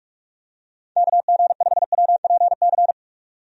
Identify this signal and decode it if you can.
Morse code, heard as a keyed tone, about 47 wpm, 700 Hz. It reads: KG5WPC
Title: USA Call Signs - 6 Characters - 45wpm